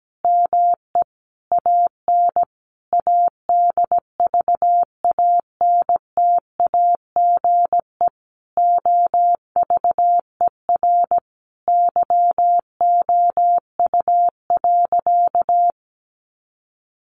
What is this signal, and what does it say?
Morse code, heard as a keyed tone, about 17 wpm, 700 Hz. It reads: ME AN ADVANTAGE OVER YOU.
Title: Sentences from Top 900 Words 17wpm